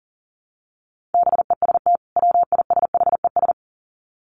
Morse code, heard as a keyed tone, about 40 wpm, 700 Hz. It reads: BEST WISHES
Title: Common QSO Phrases 40wpm